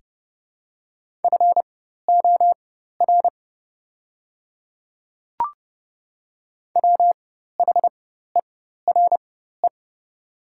Morse code, heard as a keyed tone, about 29 wpm, spaced as for 14 wpm, 700 Hz. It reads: FOR WHERE
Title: 4x Character Spacing - Top 200 Words - 30wpm